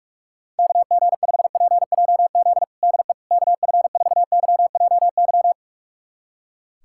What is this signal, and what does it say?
Morse code, heard as a keyed tone, about 46 wpm, 700 Hz. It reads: KG5PJZ DE KF4YJY
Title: USA Callsign DE Callsign - 6 Characters - 45wpm